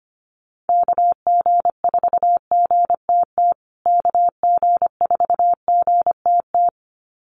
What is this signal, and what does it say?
Morse code, heard as a keyed tone, about 25 wpm, 700 Hz. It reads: KG4GTT KG4GTT
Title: USA Call Signs Repeated 2x 25wpm